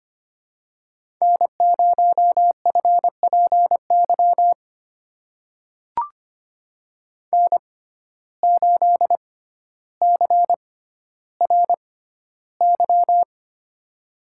Morse code, heard as a keyed tone, about 25 wpm, spaced as for 13 wpm, 700 Hz. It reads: N0FPY N8CRY
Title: USA Call Signs - 5 Characters - 6x Character Spacing - 25wpm